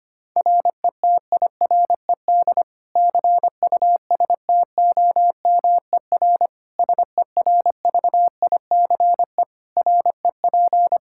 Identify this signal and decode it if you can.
Morse code, heard as a keyed tone, about 25 wpm, 700 Hz. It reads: RETIRED CUSTOMER SERVICE REP
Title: Practice QSO - Exchange 3 - More Info - 25wpm